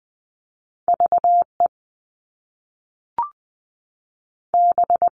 Morse code, heard as a keyed tone, about 20 wpm, 700 Hz. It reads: V E B
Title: Top Two Letter Combinations in Words 20wpm